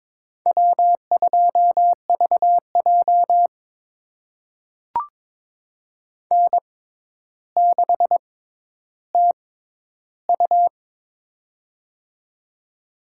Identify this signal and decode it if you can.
Morse code, heard as a keyed tone, about 22 wpm, spaced as for 11 wpm, 700 Hz. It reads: W2VJ N6TU
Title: USA Call Signs - 4 Characters - 6x Character Spacing - 22wpm